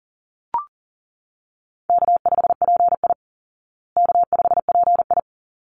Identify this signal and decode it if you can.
Morse code, heard as a keyed tone, about 40 wpm, 700 Hz. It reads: K5PI K5PI
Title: Easy Call Signs Repeated 2x - 40wpm